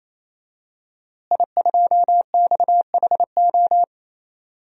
Morse code, heard as a keyed tone, about 27 wpm, 700 Hz. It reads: I2XHO
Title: DXCC-100 Call Signs - 5 Characters - Repeated 3x - 28wpm